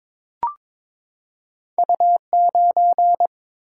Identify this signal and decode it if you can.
Morse code, heard as a keyed tone, about 23 wpm, 700 Hz. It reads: U9
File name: practice / 22wpm / Two Letter-Number Combinations - 22wpm